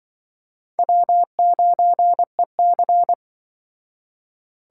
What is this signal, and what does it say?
Morse code, heard as a keyed tone, about 23 wpm, 700 Hz. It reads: W9EC
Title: Speed Racing - USA 1x2 Call Signs 20wpm